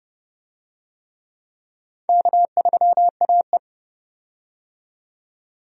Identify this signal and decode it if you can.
Morse code, heard as a keyed tone, about 32 wpm, 700 Hz. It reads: K3AE
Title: USA Call Signs - 4 Characters - 30wpm